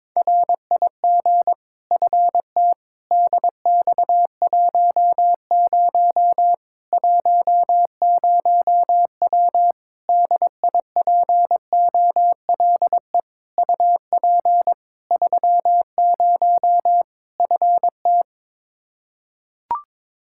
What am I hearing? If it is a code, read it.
Morse code, heard as a keyed tone, about 22 wpm, 700 Hz. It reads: RIG FT DX10 10W DIPOLE UP 30 FT
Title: Practice QSO - Exchange 2 - Info - 22wpm